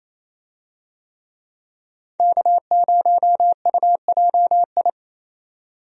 Morse code, heard as a keyed tone, about 28 wpm, 700 Hz. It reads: K0UJI
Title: USA Call Signs - 5 Characters - 28wpm